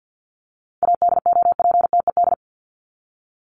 Morse code, heard as a keyed tone, about 50 wpm, 700 Hz. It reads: ADOPTED
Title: ED Suffix - Top 100 Words with Base Words 50wpm